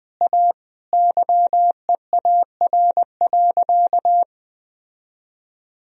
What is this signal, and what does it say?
Morse code, heard as a keyed tone, about 21 wpm, 700 Hz. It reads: A YEAR.
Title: Sentences from Top 1000 Words - Review Part 1 - 20wpm